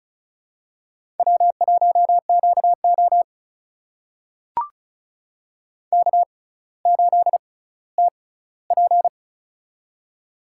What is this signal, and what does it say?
Morse code, heard as a keyed tone, about 36 wpm, spaced as for 18 wpm, 700 Hz. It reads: W1QO K8TP
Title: USA Call Signs - 4 Characters - 6x Character Spacing - 35wpm